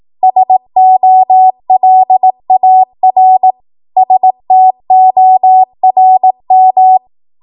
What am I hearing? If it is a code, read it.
Morse code, heard as a keyed tone, about 18 wpm, 750 Hz. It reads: SOLAR STORM
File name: Archive / 18 WPM / 18WPM